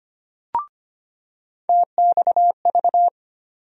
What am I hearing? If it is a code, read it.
Morse code, heard as a keyed tone, about 25 wpm, 700 Hz. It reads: TXV
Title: USA Call Sign Postfixes - 3 Letters 25wpm